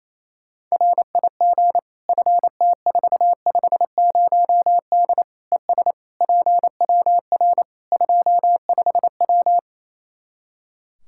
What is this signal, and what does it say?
Morse code, heard as a keyed tone, about 28 wpm, 700 Hz. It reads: RIG FT450D ES PWR 25W
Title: K1USN SST QSO Practice 28wpm